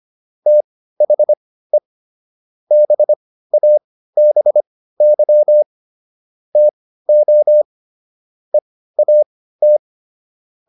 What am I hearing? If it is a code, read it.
Morse code, heard as a keyed tone, about 25 wpm, spaced as for 15 wpm, 600 Hz. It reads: THE BABY TO EAT
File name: SS119 15